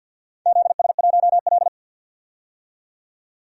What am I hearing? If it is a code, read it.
Morse code, heard as a keyed tone, about 49 wpm, 700 Hz. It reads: ZS1L
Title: DXCC-100 Call Signs - 4 Characters - 50wpm